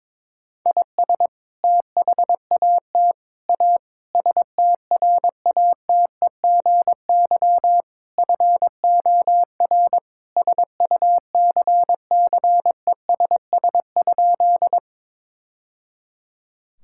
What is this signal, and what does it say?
Morse code, heard as a keyed tone, about 22 wpm, 700 Hz. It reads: IS THAT A STRATEGY FOR SUCCESS?
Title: Sentences from Top 1000 Words - Review Part 5 - 22wpm